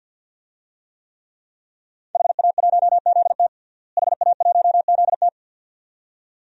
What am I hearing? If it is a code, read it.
Morse code, heard as a keyed tone, about 53 wpm, 700 Hz. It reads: HA1ZT HA1ZT
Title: DXCC-100 Call Signs - 5 Characters - Repeated 2x - 50wpm